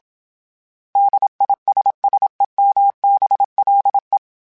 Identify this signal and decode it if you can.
Morse code, heard as a keyed tone, about 27 wpm, 800 Hz. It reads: DISSEMBLE
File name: Special INGs / Special INGs-mp3s / Suffixes @25